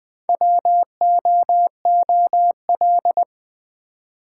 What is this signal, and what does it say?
Morse code, heard as a keyed tone, about 20 wpm, 700 Hz. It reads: WOOL
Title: Build A Word - 1 Letter At A Time - 4-Letter Words - 20wpm